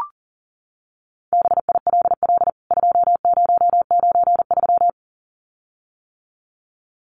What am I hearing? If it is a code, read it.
Morse code, heard as a keyed tone, about 40 wpm, 700 Hz. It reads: BILL 2093
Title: CWT Exchange 40wpm